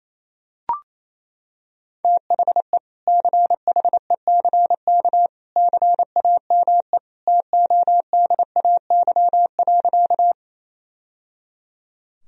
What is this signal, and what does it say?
Morse code, heard as a keyed tone, about 28 wpm, 700 Hz. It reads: THE CHECK CAME TODAY.
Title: Sentences from Top 1000 Words - Review Part 3 - 28wpm